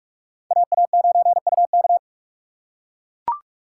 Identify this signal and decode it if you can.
Morse code, heard as a keyed tone, about 42 wpm, 700 Hz. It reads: AA9UK
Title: USA Call Signs - 5 Characters - 3x Character Spacing - 45wpm